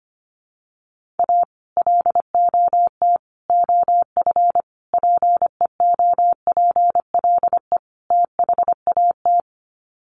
Morse code, heard as a keyed tone, about 25 wpm, 700 Hz. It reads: A LOT OF PEOPLE THAT
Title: Sets of 5 Words Encore2 25wpm